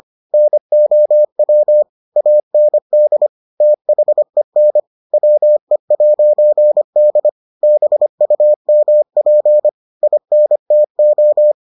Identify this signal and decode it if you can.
Morse code, heard as a keyed tone, about 25 wpm, 600 Hz. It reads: NOW AND THEN WE'D BUMP INTO